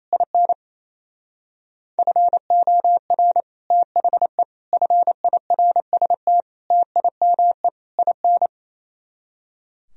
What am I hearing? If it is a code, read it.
Morse code, heard as a keyed tone, about 37 wpm, 700 Hz. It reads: IN FOR THE FIRST TIME IN